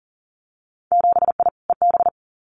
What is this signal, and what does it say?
Morse code, heard as a keyed tone, about 40 wpm, 700 Hz. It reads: 7I EB
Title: Winter Field Day Exchanges Encore 40wpm